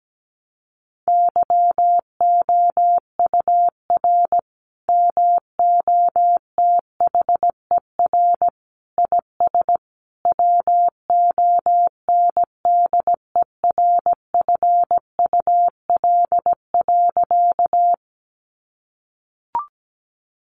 Morse code, heard as a keyed tone, about 17 wpm, 700 Hz. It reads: YOUR MOTHER IS WONDERFUL.
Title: Sentences from Top 1100 Words 17wpm